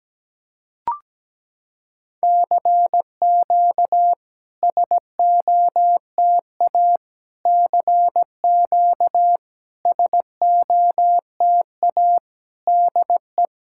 Morse code, heard as a keyed tone, about 17 wpm, 700 Hz. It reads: CQ SOTA CQ SOTA DE
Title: SOTA QSO Practice 17wpm